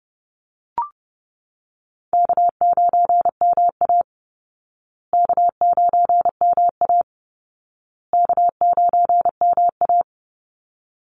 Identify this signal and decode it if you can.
Morse code, heard as a keyed tone, about 30 wpm, 700 Hz. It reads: K9MA K9MA K9MA
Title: Easy Call Signs Repeated 3x - 30wpm